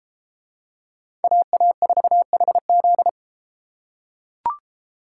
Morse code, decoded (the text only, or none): AA4HZ